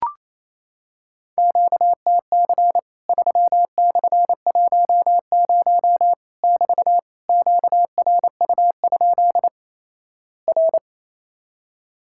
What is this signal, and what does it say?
Morse code, heard as a keyed tone, about 28 wpm, 700 Hz. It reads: QTC 3/10 <BT> QRU? R
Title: Worked All Europe DX Contest - QTC Exchange - 28wpm